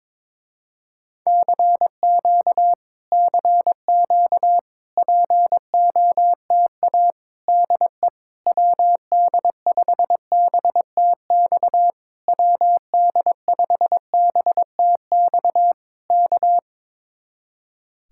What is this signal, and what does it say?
Morse code, heard as a keyed tone, about 22 wpm, 700 Hz. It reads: CQ CQ POTA DE WD5BTX WD5BTX K
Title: Speed-Racing Common QSO Phrases - 22wpm